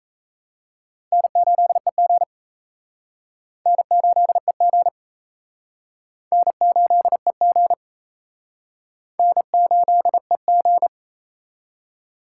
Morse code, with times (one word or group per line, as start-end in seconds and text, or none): 1.1-2.2 N8EG
3.7-4.9 N8EG
6.3-7.7 N8EG
9.2-10.9 N8EG